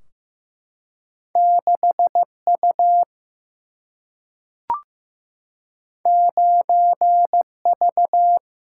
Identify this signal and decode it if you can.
Morse code, heard as a keyed tone, about 15 wpm, 700 Hz. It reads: 6U 9V
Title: Single-Letter Single-Number 15wpm